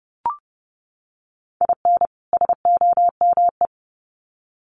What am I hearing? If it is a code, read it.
Morse code, heard as a keyed tone, about 30 wpm, 700 Hz. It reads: IN SOME
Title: Sets of 2 Words 30wpm